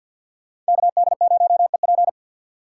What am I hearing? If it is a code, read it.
Morse code, heard as a keyed tone, about 67 wpm, spaced as for 58 wpm, 700 Hz. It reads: KD0EP
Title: Speed Racing - USA 2x2 Call Signs 50wpm